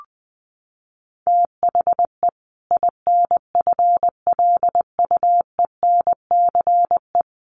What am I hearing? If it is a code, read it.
Morse code, heard as a keyed tone, about 19 wpm, 700 Hz. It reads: THE INFLUENCE
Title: Sentences from Top 1100 Words 20wpm